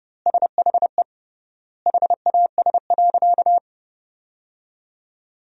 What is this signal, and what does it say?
Morse code, heard as a keyed tone, about 30 wpm, 700 Hz. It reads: SHE HAS.
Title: 3x Word-Spacing - Sentences from Top 500 Words - 30wpm